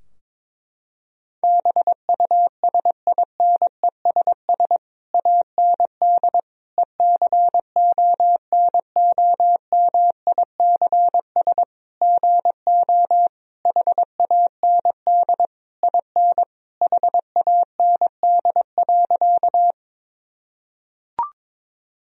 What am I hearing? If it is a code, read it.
Morse code, heard as a keyed tone, about 22 wpm, 700 Hz. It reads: BUSINESS AND ECONOMICS GO HAND IN HAND.
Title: Sentences from Top 1000 Words - Review Part 3 - 22wpm